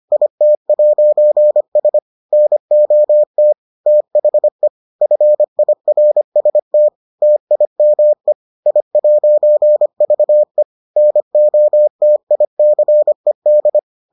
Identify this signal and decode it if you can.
Morse code, heard as a keyed tone, about 25 wpm, 600 Hz. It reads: IT'S NOT THE FIRST TIME I'VE NOTICED